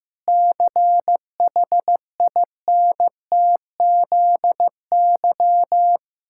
Morse code, heard as a keyed tone, about 15 wpm, 700 Hz. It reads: CHINTZY
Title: Pangram Challenge v2 - 15wpm